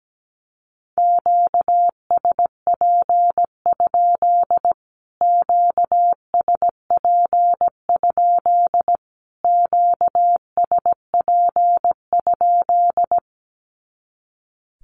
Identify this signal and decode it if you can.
Morse code, heard as a keyed tone, about 17 wpm, 700 Hz. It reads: QSP? QSP? QSP?
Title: Q-Codes Repeated 3x 17wpm